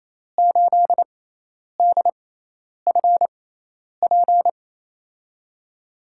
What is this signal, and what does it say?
Morse code, heard as a keyed tone, about 28 wpm, 700 Hz. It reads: 8 D F P
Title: USA Call Signs - 6x Character Spacing - 28wpm